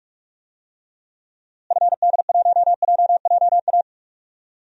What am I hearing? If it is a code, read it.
Morse code, heard as a keyed tone, about 46 wpm, 700 Hz. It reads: FD1JJA
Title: DXCC-100 Call Signs - 6 Characters - Repeated 3x - 45wpm